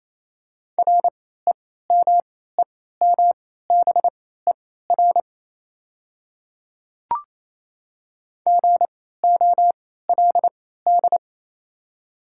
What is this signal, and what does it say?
Morse code, heard as a keyed tone, about 28 wpm, spaced as for 16 wpm, 690 Hz. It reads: REMEMBER GOLD
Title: 3x Character Spacing - Top 500 Words - 28wpm